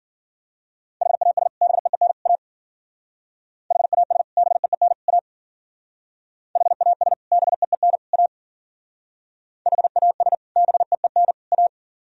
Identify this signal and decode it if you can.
Morse code, heard as a keyed tone, about 52 wpm, 700 Hz. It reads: HAS BEEN A HAS BEEN A HAS BEEN A HAS BEEN A